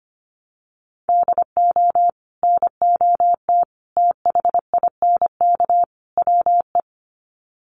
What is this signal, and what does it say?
Morse code, heard as a keyed tone, about 25 wpm, 700 Hz. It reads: DO NOT THINK WE
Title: Sets of 4 Words 25wpm